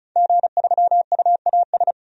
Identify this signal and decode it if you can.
Morse code, heard as a keyed tone, about 36 wpm, 700 Hz. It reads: G3UAS
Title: DXCC-100 Call Signs - 5 Characters - 35wpm